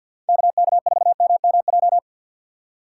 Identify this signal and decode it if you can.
Morse code, heard as a keyed tone, about 46 wpm, 700 Hz. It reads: KK4MMJ
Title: USA Call Signs - 3x Character Spacing - 50wpm